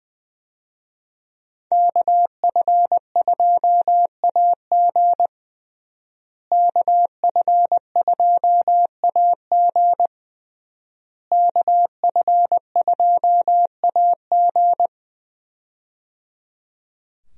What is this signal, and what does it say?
Morse code, heard as a keyed tone, about 20 wpm, 700 Hz. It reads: KF2AG KF2AG KF2AG
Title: USA Call Signs - 5 Characters - Repeated 3x - 20wpm